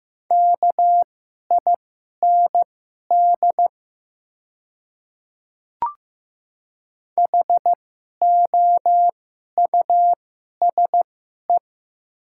Morse code, heard as a keyed tone, about 15 wpm, spaced as for 11 wpm, 700 Hz. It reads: KIND HOUSE
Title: 2x Character Spacing - Top 500 Words - 15wpm